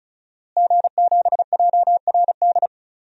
Morse code, heard as a keyed tone, about 34 wpm, 700 Hz. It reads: G7JRD
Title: DXCC-100 Call Signs - 5 Characters - Repeated 3x - 35wpm